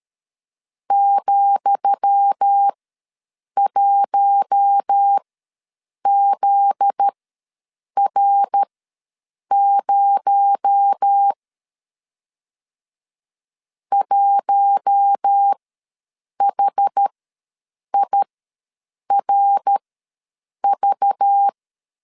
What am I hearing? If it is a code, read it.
Morse code, heard as a keyed tone, about 13 wpm, spaced as for 7 wpm, 800 Hz. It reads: ,1ZR0 1HIRV
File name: LEZIONI / 10 SIGNS / 10D COMA ws 12wpm 3618